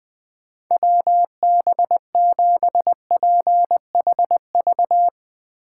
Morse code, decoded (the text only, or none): WB7PHV